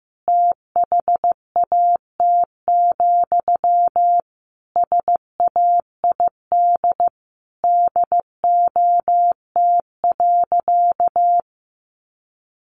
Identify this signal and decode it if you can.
Morse code, heard as a keyed tone, about 15 wpm, 700 Hz. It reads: THAT, SAID DOT.
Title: Sentences from Top 1500 Words 15wpm